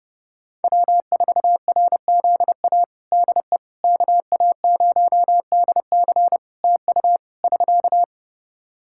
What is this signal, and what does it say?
Morse code, heard as a keyed tone, about 30 wpm, 700 Hz. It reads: W4RZA DE KA0DC TU <SK>
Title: Practice QSO - Exchange 4 - Ending - 30wpm